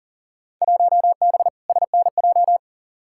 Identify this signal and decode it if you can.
Morse code, heard as a keyed tone, about 40 wpm, 700 Hz. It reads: 1B SNJ
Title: Field Day Exchanges Encore - 40wpm